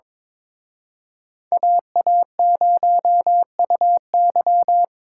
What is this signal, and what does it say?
Morse code, heard as a keyed tone, about 26 wpm, 700 Hz. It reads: AA0UY
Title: Speed Racing - USA 2x2 Call Signs 22wpm